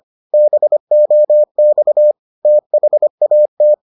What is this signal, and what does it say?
Morse code, heard as a keyed tone, about 25 wpm, 600 Hz. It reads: BOX THAT